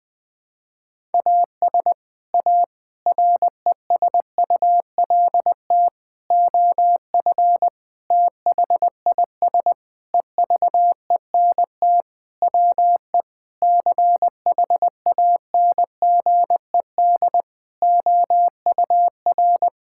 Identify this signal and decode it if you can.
Morse code, heard as a keyed tone, about 20 wpm, 700 Hz. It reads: AS A RESULT OF THIS EVENT WE CHANGED OUR
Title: Sentences with 5-Word Phrases - 20wpm